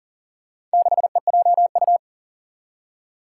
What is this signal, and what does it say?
Morse code, heard as a keyed tone, about 40 wpm, 700 Hz. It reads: 6EJU
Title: Four Challenging Letter-Number Combinations 40wpm